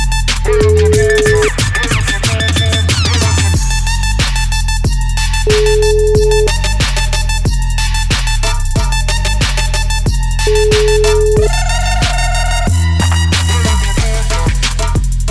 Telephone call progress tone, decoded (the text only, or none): ringback tone